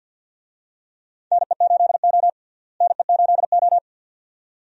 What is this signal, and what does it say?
Morse code, heard as a keyed tone, about 52 wpm, 700 Hz. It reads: NE8O NE8O
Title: USA Call Signs - 4 Characters - Repeated 2x - 50wpm